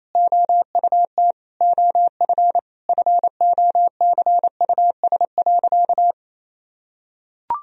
Morse code, heard as a keyed tone, about 27 wpm, 700 Hz. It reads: OUT OF FOCUS.
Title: Sentences from Top 1000 Words - Review Part 4 - 28wpm